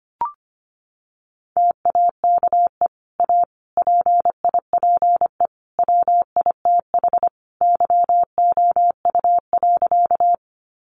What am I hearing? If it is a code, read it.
Morse code, heard as a keyed tone, about 25 wpm, 700 Hz. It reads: TAKE A PIPE WITH YOU.
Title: Sentences from Top 1500 Words 25wpm